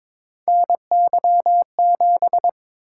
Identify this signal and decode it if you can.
Morse code, heard as a keyed tone, about 22 wpm, 700 Hz. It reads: NY7
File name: practice / 22wpm / Three Letter-Number Combinations - 22wpm